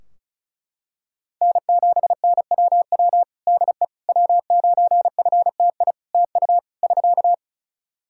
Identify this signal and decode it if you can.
Morse code, heard as a keyed tone, about 36 wpm, 700 Hz. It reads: N7NWW DE W9FTI TU <SK>
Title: Practice QSO - Exchange 4 - Ending - 35wpm